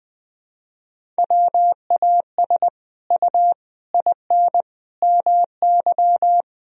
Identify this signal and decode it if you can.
Morse code, heard as a keyed tone, about 20 wpm, 700 Hz. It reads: WAS U IN MY
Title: Sentences from Top 300 Words and Abbreviations 20wpm